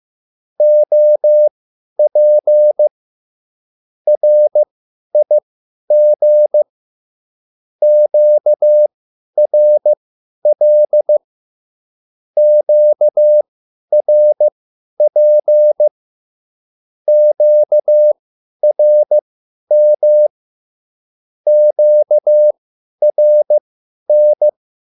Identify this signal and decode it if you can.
Morse code, heard as a keyed tone, about 15 wpm, spaced as for 10 wpm, 600 Hz. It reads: OP RIG QRL QRP QRM QRN
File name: QSL / CW lesson